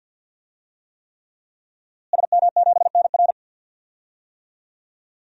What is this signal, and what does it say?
Morse code, heard as a keyed tone, about 52 wpm, 700 Hz. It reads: SM7NR
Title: DXCC-100 Call Signs - 5 Characters - Repeated 3x - 50wpm